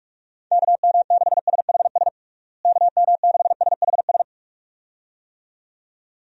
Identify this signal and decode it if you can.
Morse code, heard as a keyed tone, about 41 wpm, 700 Hz. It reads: KM6SHS KM6SHS
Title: USA Call Signs - 6 Characters - Repeated 2x - 45wpm